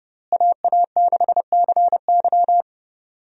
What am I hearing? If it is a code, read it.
Morse code, heard as a keyed tone, about 32 wpm, 700 Hz. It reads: AA6CY
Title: USA Call Signs - 5 Characters - Repeated 3x - 30wpm